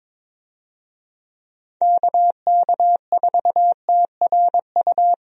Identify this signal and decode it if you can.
Morse code, heard as a keyed tone, about 22 wpm, 700 Hz. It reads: KK4TRU
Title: USA Call Signs - 6 Characters - Repeated 3x - 22wpm